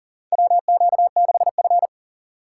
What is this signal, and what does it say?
Morse code, heard as a keyed tone, about 40 wpm, 700 Hz. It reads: WQ6F
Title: USA Call Signs - 4 Characters - 2x Character Spacing - 40wpm